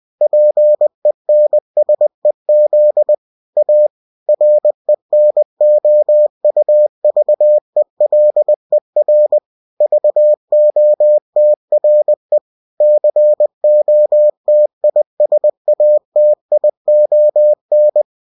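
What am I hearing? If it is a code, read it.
Morse code, heard as a keyed tone, about 20 wpm, 600 Hz. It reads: PENSEZ A RENOUVELER VOTRE COTISATION
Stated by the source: UFT bulletin sent in CW.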